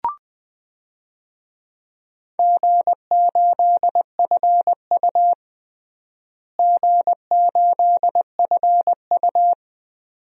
Morse code, heard as a keyed tone, about 21 wpm, 700 Hz. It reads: G8FU G8FU
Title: DXCC-100 Call Signs - 4 Characters - Repeated 2x - 20wpm